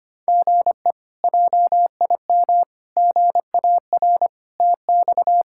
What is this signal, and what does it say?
Morse code, heard as a keyed tone, about 25 wpm, 700 Hz. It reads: GE JIM GAR TX
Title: K1USN SST QSO Practice - 25wpm